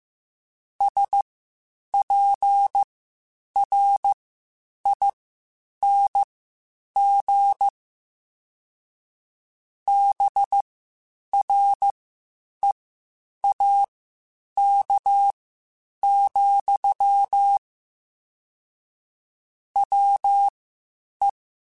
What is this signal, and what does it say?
Morse code, heard as a keyed tone, about 15 wpm, spaced as for 8 wpm, 800 Hz. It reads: SPRING BREAK, WE
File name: (spaced) / QSOs Spaced-mp3s / WA7QOX 1 @13 sp